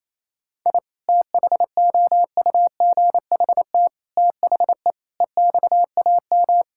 Morse code, heard as a keyed tone, about 28 wpm, 700 Hz. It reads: I THOUGHT THE EXAM
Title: Sentences from Top 1000 Words - Review Part 5 - 28wpm